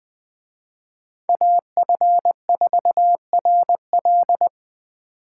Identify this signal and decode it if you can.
Morse code, heard as a keyed tone, about 24 wpm, 700 Hz. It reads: AF4RL
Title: Speed Racing - USA 2x2 Call Signs 20wpm